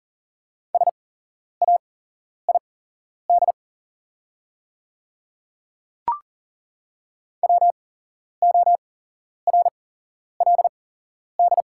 Morse code, heard as a keyed tone, about 40 wpm, spaced as for 11 wpm, 700 Hz. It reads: SAID WORLD
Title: 8x Character Spacing - Top 200 Words - 40wpm